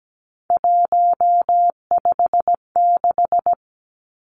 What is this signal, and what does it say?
Morse code, heard as a keyed tone, about 17 wpm, 700 Hz. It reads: 156
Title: Three Challenging Letter-Number Combinations 17wpm